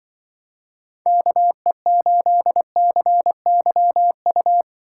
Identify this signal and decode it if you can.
Morse code, heard as a keyed tone, about 24 wpm, 700 Hz. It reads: KE8CYU
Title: Speed Racing - USA Call Signs 20wpm